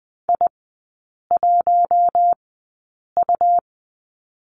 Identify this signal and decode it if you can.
Morse code, heard as a keyed tone, about 20 wpm, spaced as for 8 wpm, 700 Hz. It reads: I1U
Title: 100 Lesson 17e - Letter u - Three Characters 20wpm